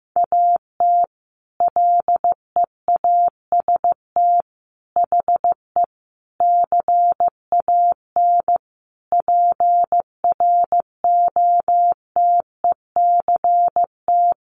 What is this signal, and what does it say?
Morse code, heard as a keyed tone, about 15 wpm, 700 Hz. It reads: AT LEAST HE CAN PROTECT
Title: Sentences from Top 900 Words 15wpm